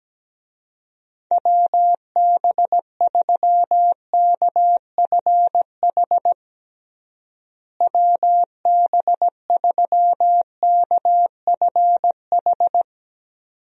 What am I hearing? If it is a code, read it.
Morse code, heard as a keyed tone, about 17 wpm, 700 Hz. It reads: WB3KFH WB3KFH
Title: USA Call Signs - 6 Characters - Repeated 2x - 17wpm